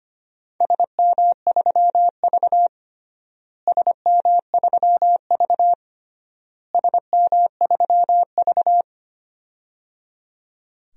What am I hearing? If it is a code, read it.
Morse code, heard as a keyed tone, about 25 wpm, 700 Hz. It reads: SM3V SM3V SM3V
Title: DXCC-100 Call Signs - 4 Characters - Repeated 3x - 25wpm